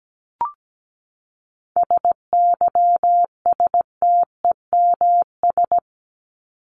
Morse code, heard as a keyed tone, about 17 wpm, 700 Hz. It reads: SYSTEMS
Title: ES:S Suffix - Top 100 Words with Base Words 17wpm